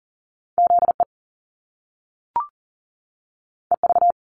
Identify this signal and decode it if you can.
Morse code, heard as a keyed tone, about 38 wpm, 700 Hz. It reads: ZE EV
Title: Top Two Letter Combinations in Words 40wpm